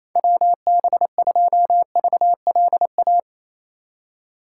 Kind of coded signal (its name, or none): Morse code